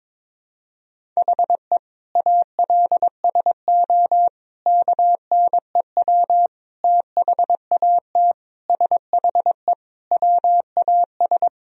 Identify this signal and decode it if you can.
Morse code, heard as a keyed tone, about 22 wpm, 700 Hz. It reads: HE ALSO KNEW THAT SHE WAS
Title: Sentences from Top 1000 Words - Review Part 1 - 22wpm